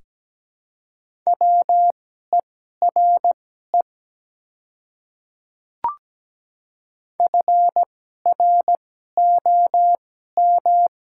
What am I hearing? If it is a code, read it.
Morse code, heard as a keyed tone, about 17 wpm, spaced as for 12 wpm, 690 Hz. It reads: WERE FROM
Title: 2x Character Spacing - Top 300 Words - 17wpm